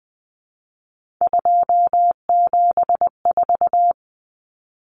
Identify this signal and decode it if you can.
Morse code, heard as a keyed tone, about 20 wpm, 700 Hz. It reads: 274
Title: Numbers - 3-Digit 20wpm